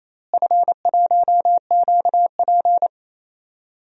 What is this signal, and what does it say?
Morse code, heard as a keyed tone, about 28 wpm, 700 Hz. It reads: F1QP
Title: DXCC-100 Call Signs - 4 Characters - Repeated 2x - 28wpm